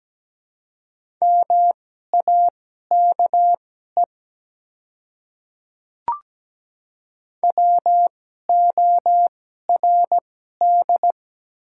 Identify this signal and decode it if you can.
Morse code, heard as a keyed tone, about 17 wpm, spaced as for 12 wpm, 700 Hz. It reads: MAKE WORD
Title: 2x Character Spacing - Top 100 Words - 17wpm